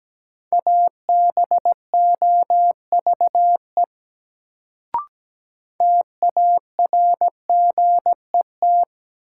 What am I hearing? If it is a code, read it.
Morse code, heard as a keyed tone, about 17 wpm, 700 Hz. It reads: ABOVE TARGET
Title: Sentences from Top 1200 Words - New Word Drill Alt - 17wpm